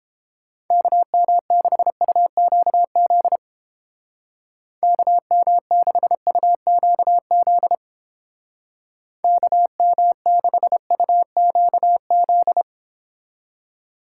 Morse code, heard as a keyed tone, about 29 wpm, 700 Hz. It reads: KM6UQZ KM6UQZ KM6UQZ